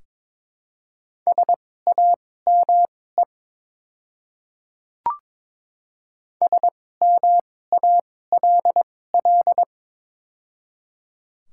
Morse code, heard as a keyed tone, about 22 wpm, spaced as for 16 wpm, 700 Hz. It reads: SAME SMALL
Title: 2x Character Spacing - Top 200 Words - 22wpm